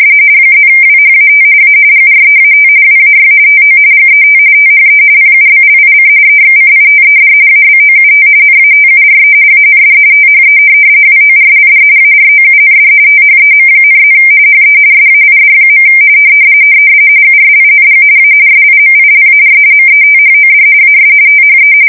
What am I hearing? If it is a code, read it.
RTTY, about 45.45 baud. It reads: +WHAT A GREAT WAY TO GET THE PUBLIC EXCITED ABOUT NATURE AND THE WORLD AROUND THEM,+ HE SAID. USING A SIMILAR METHOD AS THE
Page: 170 Shift 2125/2295